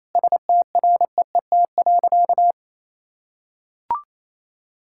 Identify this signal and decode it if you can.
Morse code, heard as a keyed tone, about 28 wpm, 700 Hz. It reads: STREET.
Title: Sentences from Top 1000 Words - Review Part 5 - 28wpm